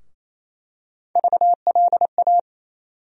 Morse code, heard as a keyed tone, about 28 wpm, 700 Hz. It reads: VLA
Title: USA Call Sign Postfixes - 3 Letters 28wpm